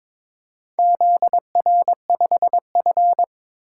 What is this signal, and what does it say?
Morse code, heard as a keyed tone, about 23 wpm, 700 Hz. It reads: ZR5F
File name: practice / 22wpm / Four Letter-Number Combinations - 22wpm